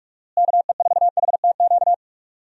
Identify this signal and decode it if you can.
Morse code, heard as a keyed tone, about 44 wpm, 700 Hz. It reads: KE4HTQ
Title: USA Call Signs - 6 Characters - Repeated 2x - 45wpm